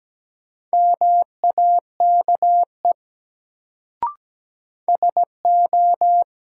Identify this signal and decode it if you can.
Morse code, heard as a keyed tone, about 17 wpm, 700 Hz. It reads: MAKE SO
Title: Sentences from Top 100 Words - New Word Drill - 17wpm